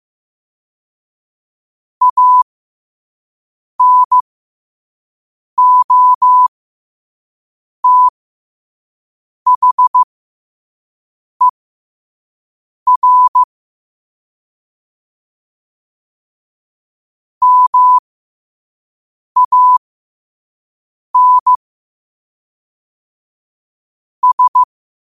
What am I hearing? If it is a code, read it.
Morse code, heard as a keyed tone, about 15 wpm, spaced as for 5 wpm, 1000 Hz. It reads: ANOTHER MAN S
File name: qotdfiles / 5wpm